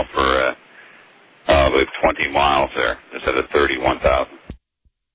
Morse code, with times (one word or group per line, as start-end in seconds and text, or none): none